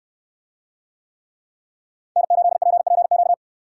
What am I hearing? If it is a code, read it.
Morse code, heard as a keyed tone, about 65 wpm, 700 Hz. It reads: N8PPQ